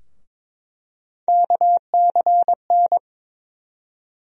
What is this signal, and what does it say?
Morse code, heard as a keyed tone, about 22 wpm, 700 Hz. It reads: KCN
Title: USA Call Sign Postfixes - 3 Letters 22wpm